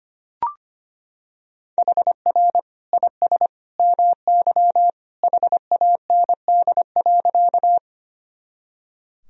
Morse code, heard as a keyed tone, about 25 wpm, 700 Hz. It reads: HR IS MY HAND.
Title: Sentences from Top 200 Words and Abbreviations 25wpm